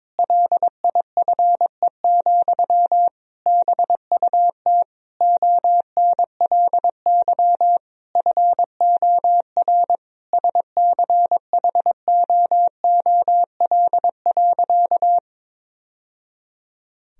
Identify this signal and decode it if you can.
Morse code, heard as a keyed tone, about 22 wpm, 700 Hz. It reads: LIFE, BUT ONLY FOR SCHOOL.
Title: Sentences from Top 1000 Words - Review Part 1 - 22wpm